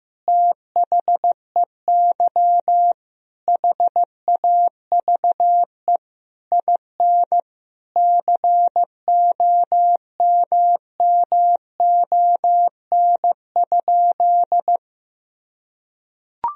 Morse code, heard as a keyed tone, about 15 wpm, 700 Hz. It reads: THEY HAVE IN COMMON?
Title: Sentences from Top 1000 Words - Review Part 3 - 15wpm